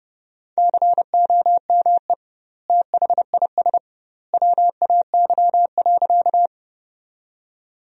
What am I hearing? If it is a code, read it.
Morse code, heard as a keyed tone, about 29 wpm, 700 Hz. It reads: COME THIS WAY.
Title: 2x Word-Spacing - Sentences from Top 100 Words - 30wpm